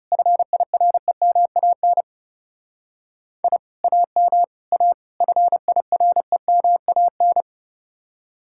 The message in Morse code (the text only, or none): FIREMAN I AM A FIREMAN